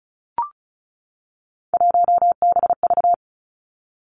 Morse code, heard as a keyed tone, about 36 wpm, 700 Hz. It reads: 1BV
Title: Three Challenging Letter-Number Combinations 35wpm